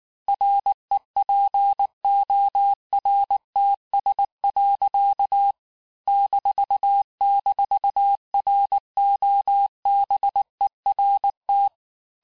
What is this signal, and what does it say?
Morse code, heard as a keyed tone, about 19 wpm, 800 Hz. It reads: REPORTS. --ROBERT
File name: cours-du-lundi files / cours-1589-texte-FIN-de-QSO-vitesse-18mpm